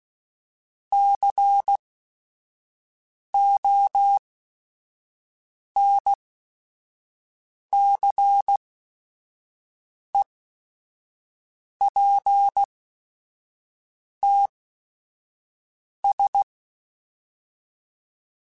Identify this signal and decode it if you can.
Morse code, heard as a keyed tone, about 16 wpm, spaced as for 5 wpm, 800 Hz. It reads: CONCEPTS
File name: Technical Subjects